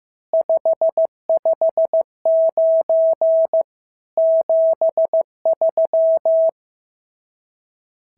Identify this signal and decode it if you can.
Morse code, heard as a keyed tone, about 15 wpm, 650 Hz. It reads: 559 73
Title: SOTA QSO Practice 15wpm